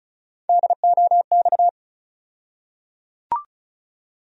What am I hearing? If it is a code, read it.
Morse code, heard as a keyed tone, about 36 wpm, 700 Hz. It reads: DOX
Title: USA Call Sign Postfixes - 3 Letters 35wpm